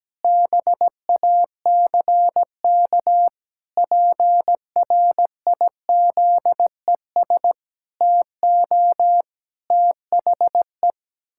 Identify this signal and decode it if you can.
Morse code, heard as a keyed tone, about 17 wpm, 700 Hz. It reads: BACK PRIZES TO THE
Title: Pangram Challenge v2 - 17wpm